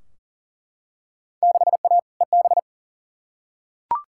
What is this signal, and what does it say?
Morse code, heard as a keyed tone, about 40 wpm, 700 Hz. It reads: 6A EB
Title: Field Day Exchanges Encore - 40wpm